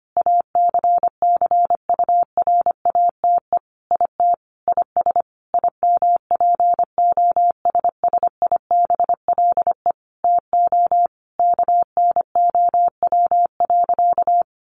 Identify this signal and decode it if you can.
Morse code, heard as a keyed tone, about 25 wpm, 700 Hz. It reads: ACCURATE IT IS IMPOSSIBLE TO KNOW.
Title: Sentences from Top 1200 Words 25wpm